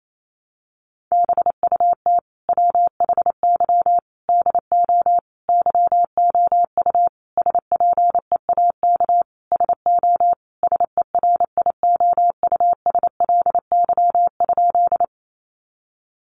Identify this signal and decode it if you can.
Morse code, heard as a keyed tone, about 28 wpm, 700 Hz. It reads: BUT WHY DO YOU SPEAK SO SERIOUSLY?
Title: Sentences from Top 1200 Words 28wpm